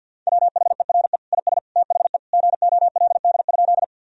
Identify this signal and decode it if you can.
Morse code, heard as a keyed tone, about 50 wpm, 700 Hz. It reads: WHERE IS THE GOLD?
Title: Sentences from Top 1000 Words - Review Part 3 - 50wpm